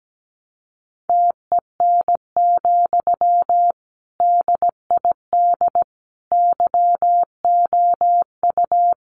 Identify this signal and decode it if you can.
Morse code, heard as a keyed tone, about 17 wpm, 700 Hz. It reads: TEN, DID YOU
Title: Sentences from Top 400 Words 17wpm